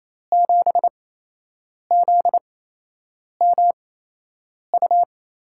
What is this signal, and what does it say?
Morse code, heard as a keyed tone, about 28 wpm, spaced as for 8 wpm, 700 Hz. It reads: 7ZMU
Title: USA Call Signs - 6 Characters - 8x Character Spacing - 28wpm